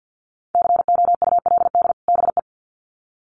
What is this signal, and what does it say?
Morse code, heard as a keyed tone, about 50 wpm, 700 Hz. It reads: COULD BE
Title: Sets of 2 Words 50wpm